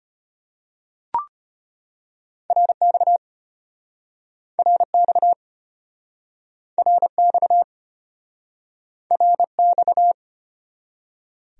Morse code, heard as a keyed tone, about 33 wpm, 700 Hz. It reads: RX RX RX RX